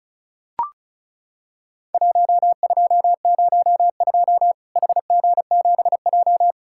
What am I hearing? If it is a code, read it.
Morse code, heard as a keyed tone, about 36 wpm, 700 Hz. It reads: 1202 HG7J
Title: Worked All Europe DX Contest - QTC Exchange - 35wpm